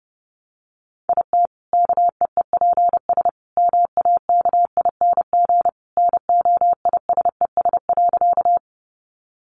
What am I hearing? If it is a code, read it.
Morse code, heard as a keyed tone, about 29 wpm, 700 Hz. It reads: IT KEEPS MAKING NOISES.
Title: Additional Easy Phrases 30wpm